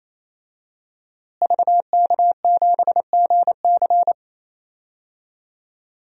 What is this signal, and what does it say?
Morse code, heard as a keyed tone, about 28 wpm, 700 Hz. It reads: VK7GC
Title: DXCC-100 Call Signs - 5 Characters - Repeated 3x - 28wpm